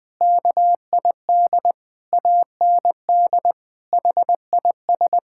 Morse code, heard as a keyed tone, about 19 wpm, 700 Hz. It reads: KID AND HIS
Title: Sentences from Top 1600 Words 20wpm